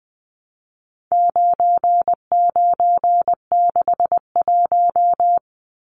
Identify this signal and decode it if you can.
Morse code, heard as a keyed tone, about 20 wpm, 700 Hz. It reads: 9961
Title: Numbers - 4-Digit 20wpm